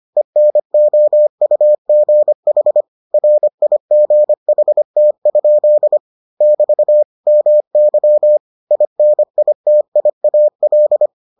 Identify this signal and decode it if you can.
Morse code, heard as a keyed tone, about 25 wpm, 600 Hz. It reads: ENOUGH RIGHT? <BT> MY INITIAL